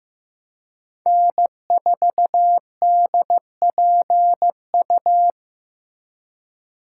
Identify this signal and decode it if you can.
Morse code, heard as a keyed tone, about 18 wpm, 700 Hz. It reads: N4DPU
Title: Speed Racing - USA 1x3 Call Signs 15wpm